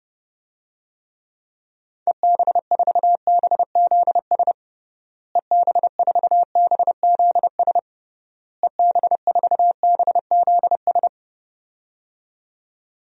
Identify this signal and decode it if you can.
Morse code, heard as a keyed tone, about 30 wpm, 700 Hz. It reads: EB4BZS EB4BZS EB4BZS
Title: DXCC-100 Call Signs - 6 Characters - Repeated 3x - 30wpm